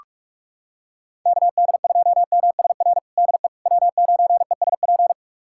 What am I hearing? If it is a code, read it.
Morse code, heard as a keyed tone, about 48 wpm, 700 Hz. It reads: KD2MSR DE W9ESP
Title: USA Callsign DE Callsign 45wpm